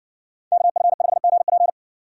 Morse code, heard as a keyed tone, about 60 wpm, 700 Hz. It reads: KF5GP